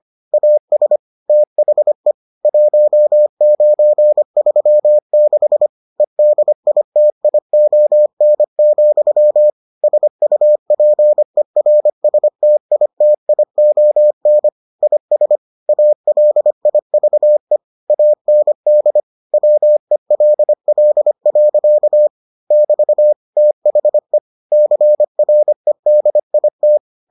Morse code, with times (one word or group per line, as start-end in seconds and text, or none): 0.3-1.0 AS
1.3-2.1 THE
2.5-5.7 1936
6.0-9.5 EDITION,
9.8-14.5 SUPERSTITION
14.8-15.4 IS
15.7-17.6 ALIVE
17.9-19.0 AND
19.3-22.1 WELL.
22.5-23.1 <BT>
23.4-24.2 THE
24.5-26.8 CREDIT